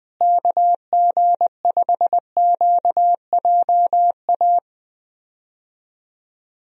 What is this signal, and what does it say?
Morse code, heard as a keyed tone, about 21 wpm, 700 Hz. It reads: KG5QJA
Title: USA Call Signs - 6 Characters - Repeated 3x - 20wpm